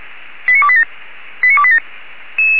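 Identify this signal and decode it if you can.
Five-tone selective call (ZVEI), heard as quick runs of five tones, tones of about 70 ms. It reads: 89287 89287
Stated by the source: Five-tone ZVEI selective call received on an SDR, from the fire brigade Feuerwehr Orlamünde audio server.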